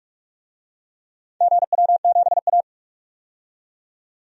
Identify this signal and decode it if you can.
Morse code, heard as a keyed tone, about 47 wpm, 700 Hz. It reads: GW7A
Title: DXCC-100 Call Signs - 4 Characters - 45wpm